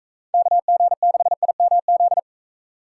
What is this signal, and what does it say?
Morse code, heard as a keyed tone, about 41 wpm, 700 Hz. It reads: KG6IMZ